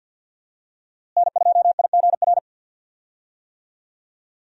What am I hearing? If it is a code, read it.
Morse code, heard as a keyed tone, about 45 wpm, 700 Hz. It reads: N2IGR
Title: USA Call Signs - 5 Characters - Repeated 2x - 50wpm